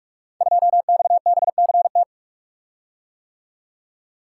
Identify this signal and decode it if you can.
Morse code, heard as a keyed tone, about 44 wpm, 700 Hz. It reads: 2XBCT
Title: Code Groups - 5 Letter:Numbers - 45wpm